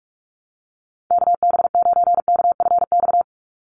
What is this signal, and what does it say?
Morse code, heard as a keyed tone, about 44 wpm, 700 Hz. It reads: KB9KFX
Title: USA Call Signs 45wpm